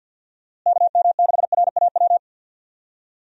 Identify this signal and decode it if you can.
Morse code, heard as a keyed tone, about 51 wpm, 700 Hz. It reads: KM6RAW
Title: USA Call Signs - 6 Characters - Repeated 3x - 50wpm